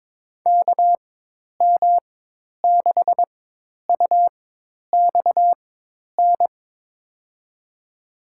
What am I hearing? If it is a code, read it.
Morse code, heard as a keyed tone, about 22 wpm, spaced as for 10 wpm, 700 Hz. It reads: KM6UXN
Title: USA Call Signs - 6 Characters - 4x Character Spacing - 22wpm